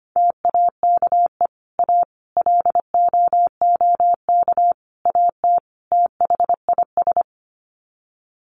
Morse code, heard as a keyed tone, about 25 wpm, 700 Hz. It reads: TAKE A LOOK AT THIS
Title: Sets of 5 Words 25wpm